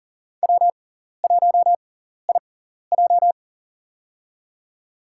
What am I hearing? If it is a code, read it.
Morse code, heard as a keyed tone, about 40 wpm, spaced as for 14 wpm, 700 Hz. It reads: W1IJ
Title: USA Call Signs - 4 Characters - 6x Character Spacing - 40wpm